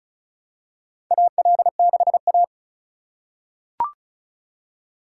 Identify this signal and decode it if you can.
Morse code, heard as a keyed tone, about 36 wpm, 700 Hz. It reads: AL6A
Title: USA Call Signs - 4x Character Spacing - 35wpm